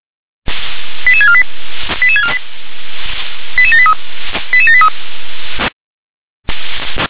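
Five-tone selective call (ZVEI), heard as quick runs of five tones, tones of about 70 ms. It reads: 88648 88648 88783 88783
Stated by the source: ZVEI five-tone selective call received by SDR.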